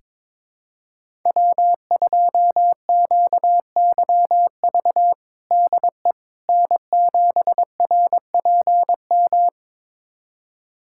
Morse code, heard as a keyed tone, about 21 wpm, 700 Hz. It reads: W2QYV DE N7RPM
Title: USA Callsign DE Callsign - 5 Characters - 22wpm